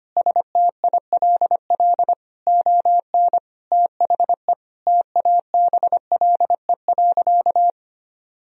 Morse code, heard as a keyed tone, about 25 wpm, 700 Hz. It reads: STILL ON THE TABLE.
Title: Sentences from Top 1000 Words - Review Part 4 - 25wpm